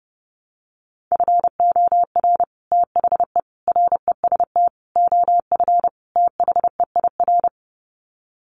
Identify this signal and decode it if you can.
Morse code, heard as a keyed tone, about 30 wpm, 700 Hz. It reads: FOR THE REST OF THEIR
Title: Sets of 5 Words Encore2 30wpm